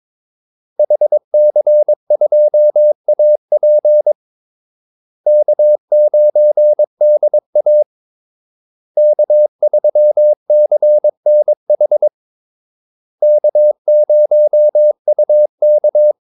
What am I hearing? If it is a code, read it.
Morse code, heard as a keyed tone, about 22 wpm, 600 Hz. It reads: HC2AP K9DA K3CNH K0UK
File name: Callsigns-22wpm